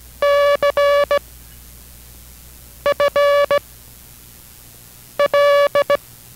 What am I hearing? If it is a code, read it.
Morse code, heard as a keyed tone, about 17 wpm, 570 Hz. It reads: C F L